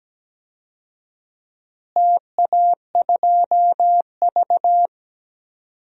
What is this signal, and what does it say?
Morse code, heard as a keyed tone, about 17 wpm, 700 Hz. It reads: TA2V
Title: DXCC-100 Call Signs - 4 Characters - Repeated 2x - 17wpm